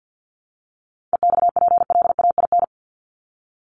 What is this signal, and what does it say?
Morse code, heard as a keyed tone, about 50 wpm, 700 Hz. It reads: EXPLAIN
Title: Top 1001-1250 Words 50wpm